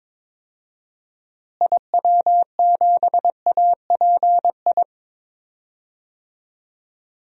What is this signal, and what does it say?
Morse code, heard as a keyed tone, about 22 wpm, 700 Hz. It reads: IW7API
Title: DXCC-100 Call Signs - 6 Characters - Repeated 3x - 22wpm